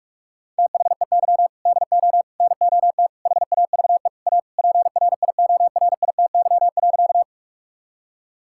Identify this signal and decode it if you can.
Morse code, heard as a keyed tone, about 46 wpm, 700 Hz. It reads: THEY DO NOT HAVE A PRIORITY.
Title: Sentences from Top 1600 Words 45wpm